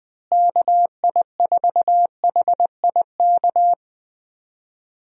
Morse code, heard as a keyed tone, about 20 wpm, 700 Hz. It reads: KI4HIK
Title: USA Call Signs - 6 Characters - Repeated 3x - 20wpm